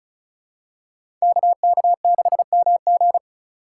Morse code, heard as a keyed tone, about 36 wpm, 700 Hz. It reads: KK6MG
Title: USA Call Signs - 5 Characters - 3x Character Spacing - 35wpm